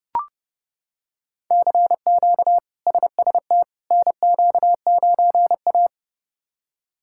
Morse code, heard as a keyed tone, about 31 wpm, 700 Hz. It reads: CQ SST NQ9A
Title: K1USN SST QSO Practice - 30wpm